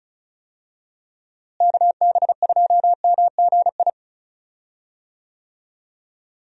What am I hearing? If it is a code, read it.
Morse code, heard as a keyed tone, about 35 wpm, 700 Hz. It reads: KB2MGI
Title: USA Call Signs - 6 Characters - Repeated 2x - 35wpm